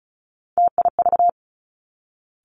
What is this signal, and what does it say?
Morse code, heard as a keyed tone, about 36 wpm, 700 Hz. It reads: TIV
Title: Three Challenging Letter-Number Combinations 35wpm